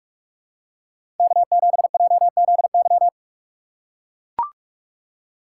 Morse code, heard as a keyed tone, about 42 wpm, 700 Hz. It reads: K7JZY
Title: USA Call Signs - 2x Character Spacing - 45wpm